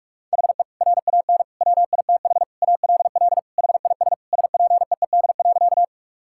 Morse code, heard as a keyed tone, about 41 wpm, 700 Hz. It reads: HE RAN WITH ALL HIS SPEED.
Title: Sentences from Top 1000 Words - Review Part 5 - 45wpm